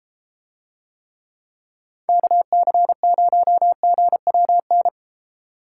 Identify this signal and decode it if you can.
Morse code, heard as a keyed tone, about 30 wpm, 700 Hz. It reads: KC0GWN